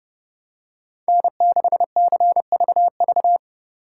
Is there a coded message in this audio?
Yes: Morse code from 1.1 to 3.4 s, heard as a keyed tone, about 29 wpm, 700 Hz. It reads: N6CVV